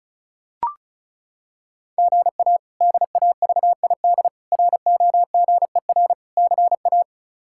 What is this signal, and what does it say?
Morse code, heard as a keyed tone, about 36 wpm, 700 Hz. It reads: GA DAVID ROGER CA
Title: K1USN SST QSO Practice - 35wpm